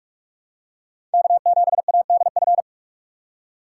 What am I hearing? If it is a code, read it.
Morse code, heard as a keyed tone, about 45 wpm, 700 Hz. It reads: K7ADF
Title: USA Call Signs - 5 Characters - 4x Character Spacing - 45wpm